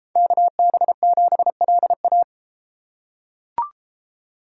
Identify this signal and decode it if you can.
Morse code, heard as a keyed tone, about 33 wpm, 700 Hz. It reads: KB7LA